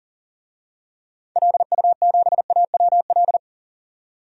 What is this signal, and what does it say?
Morse code, heard as a keyed tone, about 40 wpm, 700 Hz. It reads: LU7AWL
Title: DXCC-100 Call Signs - 6 Characters - 40wpm